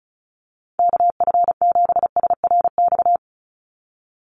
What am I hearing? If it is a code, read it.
Morse code, heard as a keyed tone, about 36 wpm, 700 Hz. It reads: KF7SRX
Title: USA Call Signs 35wpm